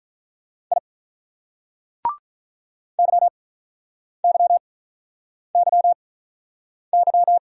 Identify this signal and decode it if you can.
Morse code, heard as a keyed tone, about 48 wpm, spaced as for 9 wpm, 700 Hz. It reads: I YYYY